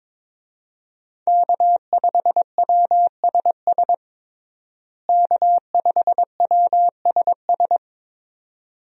Morse code, heard as a keyed tone, about 22 wpm, 700 Hz. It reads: K5WSS K5WSS
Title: USA Call Signs - 5 Characters - Repeated 2x - 22wpm